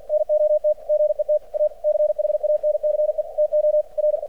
Morse code, heard as a keyed tone, about 48 wpm, 600 Hz. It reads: NOT GET A CHANCE TO R
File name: archived / CRSnet